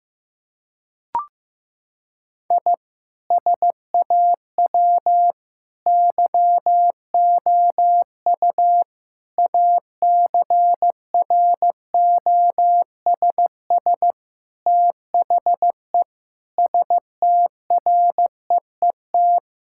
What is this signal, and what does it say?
Morse code, heard as a keyed tone, about 16 wpm, 700 Hz. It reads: I SAW YOU ACROSS THE STREET
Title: Sentences from Top 1000 Words - Review Part 5 - 15wpm